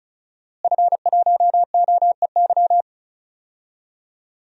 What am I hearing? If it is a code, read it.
Morse code, heard as a keyed tone, about 34 wpm, 700 Hz. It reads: F1OEY
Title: DXCC-100 Call Signs - 5 Characters - Repeated 3x - 35wpm